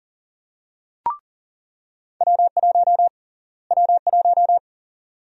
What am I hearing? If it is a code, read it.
Morse code, heard as a keyed tone, about 40 wpm, 700 Hz. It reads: W1 W1
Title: USA Call Sign Prefixes - Repeated 2x - 40wpm